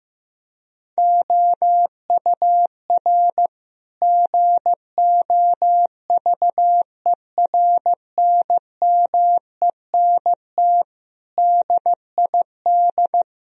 Morse code, heard as a keyed tone, about 15 wpm, 700 Hz. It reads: OUR GOVERNMENT DID
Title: Sentences from Top 1000 Words - Review Part 3 - 15wpm